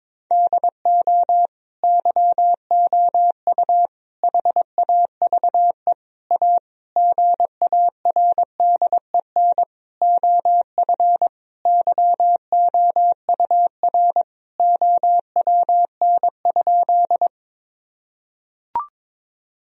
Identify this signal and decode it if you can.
Morse code, heard as a keyed tone, about 22 wpm, 700 Hz. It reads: DO YOU HAVE A GARDEN OF YOUR OWN?
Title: Sentences from Top 1000 Words - Review Part 4 - 22wpm